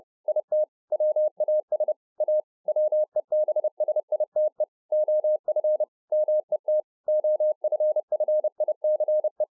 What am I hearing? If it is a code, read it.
Morse code, heard as a keyed tone, about 30 wpm, 610 Hz. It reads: IT WAS A WEBSITE OF MET OFFICE